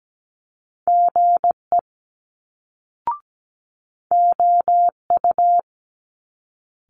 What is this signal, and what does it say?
Morse code, heard as a keyed tone, about 17 wpm, 700 Hz. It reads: GE OU
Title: Multi-letter Phonemes Quick Fire 17wpm